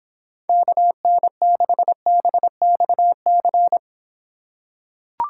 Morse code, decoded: KN6BXC

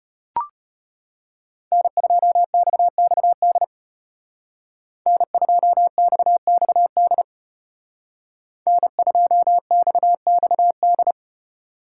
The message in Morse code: N2XXD N2XXD N2XXD

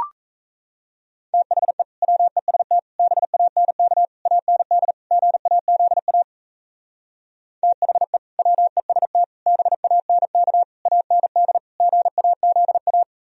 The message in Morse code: THE WEST BANK AND GAZA THE WEST BANK AND GAZA

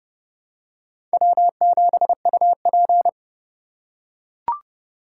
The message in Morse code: W7UP

